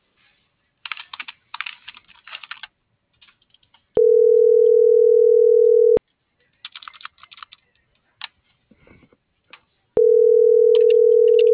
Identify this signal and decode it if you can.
Ringback tone (North America), 440 + 480 Hz.